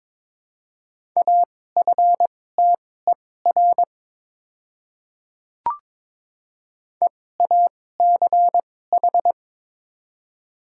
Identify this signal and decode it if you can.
Morse code, heard as a keyed tone, about 22 wpm, spaced as for 16 wpm, 700 Hz. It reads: AFTER EACH
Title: 2x Character Spacing - Top 200 Words - 22wpm